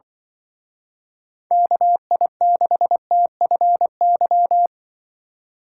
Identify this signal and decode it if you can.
Morse code, heard as a keyed tone, about 24 wpm, 700 Hz. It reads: KI6TFY
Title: Speed Racing - USA 2x3 Call Signs 20wpm